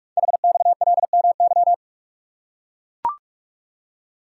Morse code, heard as a keyed tone, about 44 wpm, 690 Hz. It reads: HXLMY
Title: Code Groups - 5 Letters - 45wpm